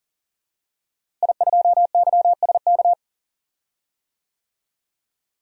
Morse code, heard as a keyed tone, about 40 wpm, 700 Hz. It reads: I2YSK
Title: DXCC-100 Call Signs - 5 Characters - Repeated 2x - 40wpm